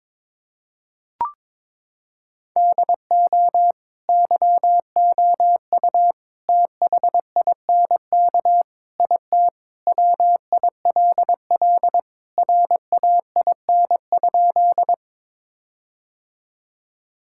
Morse code, heard as a keyed tone, about 22 wpm, 700 Hz. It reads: DO YOU THINK IT WILL RAIN?
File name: practice / 22wpm / Sentences with 4-Word Phrases - 22wpm